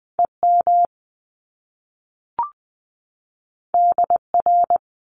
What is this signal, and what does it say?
Morse code, heard as a keyed tone, about 20 wpm, 700 Hz. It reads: EM DR
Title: Top Two Letter Combinations in Words 20wpm